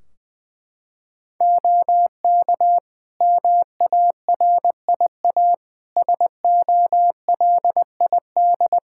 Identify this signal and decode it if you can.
Morse code, heard as a keyed tone, about 21 wpm, 700 Hz. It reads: OK MARIA SOLID
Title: Practice QSO - Exchange 3 - More Info - 20wpm